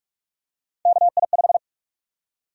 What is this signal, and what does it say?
Morse code, heard as a keyed tone, about 46 wpm, 700 Hz. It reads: KI5
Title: USA Call Sign Prefixes - 45wpm